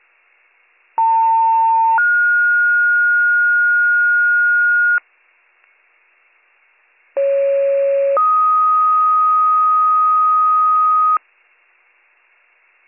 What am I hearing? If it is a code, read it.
Two-tone page, heard as a one-second tone, then a three-second tone, about 900 then 1430 Hz, about 550 then 1250 Hz. It is heard from 1.0 to 11.2 s.